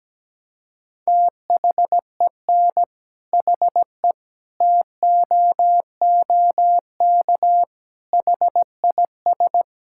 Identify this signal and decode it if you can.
Morse code, heard as a keyed tone, about 17 wpm, 700 Hz. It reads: THEN HE TOOK HIS
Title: Sentences from Top 1600 Words 17wpm